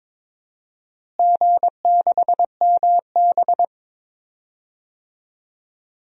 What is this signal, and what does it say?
Morse code, heard as a keyed tone, about 22 wpm, 700 Hz. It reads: G6MB
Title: DXCC-100 Call Signs - 4 Characters - Repeated 2x - 22wpm